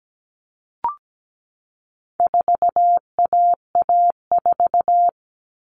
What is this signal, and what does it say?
Morse code, heard as a keyed tone, about 17 wpm, 700 Hz. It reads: 4AA4
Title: All Cut Numbers - 4-Digit 17wpm